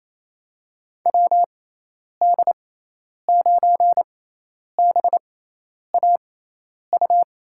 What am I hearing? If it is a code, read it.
Morse code, heard as a keyed tone, about 28 wpm, spaced as for 10 wpm, 700 Hz. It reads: WD9BAU
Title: USA Call Signs - 6 Characters - 6x Character Spacing - 28wpm